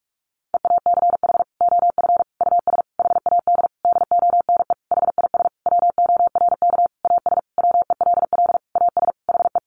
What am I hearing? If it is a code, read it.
Morse code, heard as a keyed tone, about 44 wpm, 700 Hz. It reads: EACH OF US HAD DONE HIS WORK AS WELL AS HE
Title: Sentences from Top 500 Words 45wpm